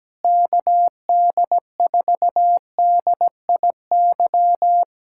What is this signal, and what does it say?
Morse code, heard as a keyed tone, about 17 wpm, 700 Hz. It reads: KD4DIY
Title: USA Callsign DE Callsign - 6 Characters - 17wpm